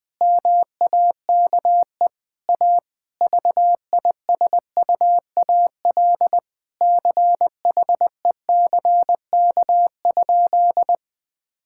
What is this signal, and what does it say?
Morse code, heard as a keyed tone, about 21 wpm, 700 Hz. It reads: MAKE A VISUAL CHECK?
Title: Sentences from Top 1000 Words - Review Part 6 - 20wpm